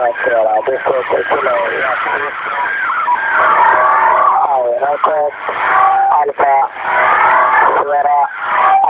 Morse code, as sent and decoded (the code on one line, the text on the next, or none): none